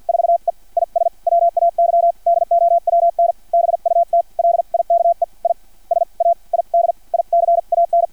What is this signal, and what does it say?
Morse code, heard as a keyed tone, about 50 wpm, spaced as for 41 wpm, 680 Hz. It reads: 4E IS WAY DOWN BUT LIKE I SAID ICAN
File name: archived / CRSnet